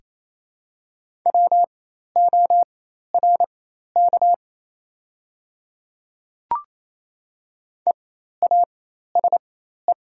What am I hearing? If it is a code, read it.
Morse code, heard as a keyed tone, about 28 wpm, spaced as for 13 wpm, 700 Hz. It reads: WORK EASE
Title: 4x Character Spacing - Top 300 Words - 28wpm